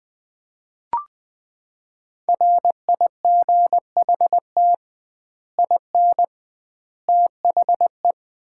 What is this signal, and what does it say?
Morse code, heard as a keyed tone, about 20 wpm, 700 Hz. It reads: RIGHT IN THE
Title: 2x Word-Spacing - Sets of 5 Words - 20wpm